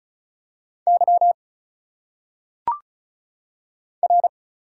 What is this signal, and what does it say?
Morse code, heard as a keyed tone, about 36 wpm, 700 Hz. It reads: Y R
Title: Single Letters - Phonetic Pronunciation 35wpm